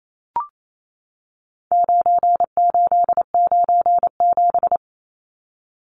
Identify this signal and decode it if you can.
Morse code, heard as a keyed tone, about 28 wpm, 700 Hz. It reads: 9897
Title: Numbers - 4-Digit 28wpm